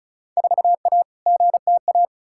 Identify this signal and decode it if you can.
Morse code, heard as a keyed tone, about 34 wpm, 700 Hz. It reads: 4A GTA
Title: Field Day Exchanges Encore - 35wpm